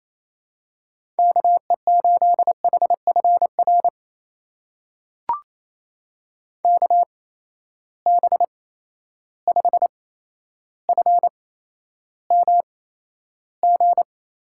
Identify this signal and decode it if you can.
Morse code, heard as a keyed tone, about 28 wpm, spaced as for 12 wpm, 700 Hz. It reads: KE8HFR KB5FMG
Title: USA Call Signs - 6 Characters - 8x Character Spacing - 28wpm